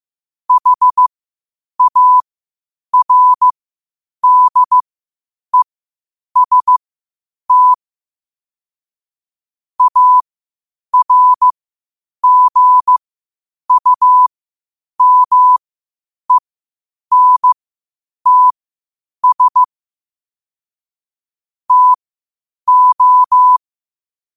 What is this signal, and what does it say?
Morse code, heard as a keyed tone, about 15 wpm, spaced as for 8 wpm, 1000 Hz. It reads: HARDEST ARGUMENTS TO
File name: qotdfiles / 8wpm